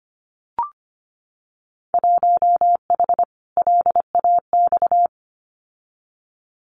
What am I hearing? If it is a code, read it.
Morse code, heard as a keyed tone, about 25 wpm, 700 Hz. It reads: 1H LAX
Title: Winter Field Day Exchanges 25wpm